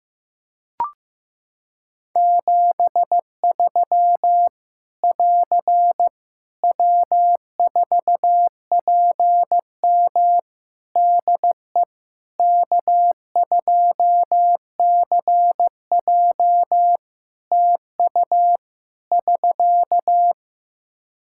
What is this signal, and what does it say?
Morse code, heard as a keyed tone, about 15 wpm, 700 Hz. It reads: 73 <AR> W4PM DE K2CJ TU <SK>
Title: K1USN SST QSO Practice 15wpm